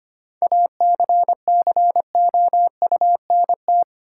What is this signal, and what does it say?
Morse code, heard as a keyed tone, about 25 wpm, 700 Hz. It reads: ACCOUNT
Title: Sentences from Top 900 Words - New Word Drill - 25wpm